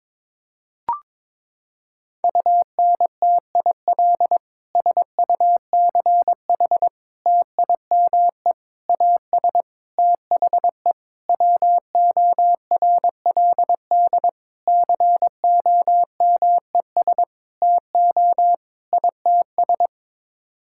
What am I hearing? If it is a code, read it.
Morse code, heard as a keyed tone, about 22 wpm, 700 Hz. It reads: UNTIL SUCH TIME AS THE WORLD COMES TO ITS
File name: Sets of 9 Words - 22wpm